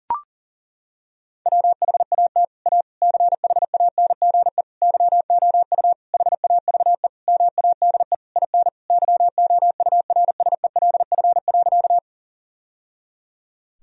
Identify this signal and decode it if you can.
Morse code, heard as a keyed tone, about 40 wpm, 700 Hz. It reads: WHAT A CHANGE YOU HAVE MADE IN YOURSELF.
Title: Sentences from Top 1000 Words - Review Part 1 - 40wpm